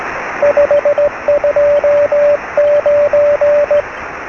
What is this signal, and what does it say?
Morse code, heard as a keyed tone, about 16 wpm, 580 Hz, with radio noise. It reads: 529
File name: sm7lcb / 0710 sm6afv 13cm